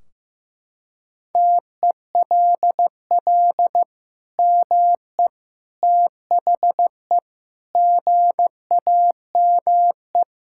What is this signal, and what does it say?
Morse code, heard as a keyed tone, about 15 wpm, 700 Hz. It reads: TELL ME THE GAME
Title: Sentences from Top 1600 Words 15wpm